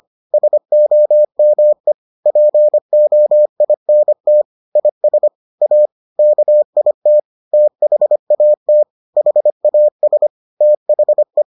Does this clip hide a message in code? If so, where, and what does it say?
Yes: Morse code from 0.3 to 11.4 s, heard as a keyed tone, about 25 wpm, 600 Hz. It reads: SOME POINT IS A KIT THAT HAS THE